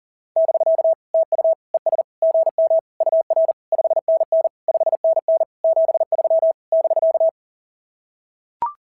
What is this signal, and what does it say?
Morse code, heard as a keyed tone, about 37 wpm, 660 Hz. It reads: <BK> TU ES GM UR 5NN 5NN 73 <BK>
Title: SOTA QSO Practice 40wpm